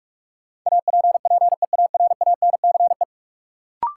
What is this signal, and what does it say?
Morse code, heard as a keyed tone, about 47 wpm, 700 Hz. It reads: APPEARANCE
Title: Sentences from Top 1100 Words - New Word Drill Alt - 45wpm